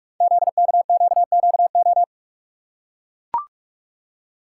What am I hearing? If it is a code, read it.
Morse code, heard as a keyed tone, about 44 wpm, 690 Hz. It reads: ZKQQO
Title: Code Groups - 5 Letters - 45wpm